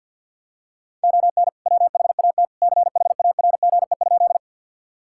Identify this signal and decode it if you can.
Morse code, heard as a keyed tone, about 52 wpm, 700 Hz. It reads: ON WHAT CHARGE?
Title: Sentences from Top 1000 Words - Review Part 6 - 50wpm